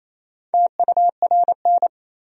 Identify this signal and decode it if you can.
Morse code, heard as a keyed tone, about 28 wpm, 700 Hz. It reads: TURN
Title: 4x Word-Spacing - Sentences from Top 200 Words - 28wpm